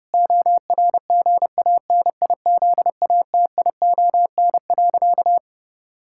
Morse code, heard as a keyed tone, about 32 wpm, 700 Hz. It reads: ORGANIZATION.
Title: Sentences from Top 1000 Words - Review Part 5 - 30wpm